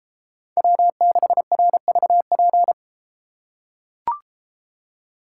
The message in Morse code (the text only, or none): W6RVP